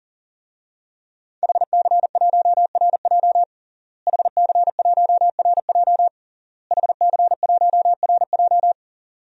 Morse code, heard as a keyed tone, about 37 wpm, 700 Hz. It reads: HC1RJ HC1RJ HC1RJ
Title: DXCC-100 Call Signs - 5 Characters - Repeated 3x - 40wpm